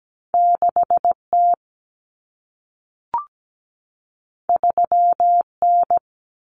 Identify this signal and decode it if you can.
Morse code, heard as a keyed tone, about 17 wpm, 700 Hz. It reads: 6T 3N
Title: Common Cut Numbers - 2-Digit 17wpm